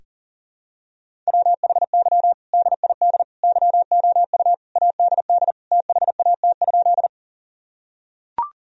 Morse code, heard as a keyed tone, about 40 wpm, 700 Hz. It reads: WHY DID YOU ADD THAT?
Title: Sentences from Top 1000 Words - Review Part 1 - 40wpm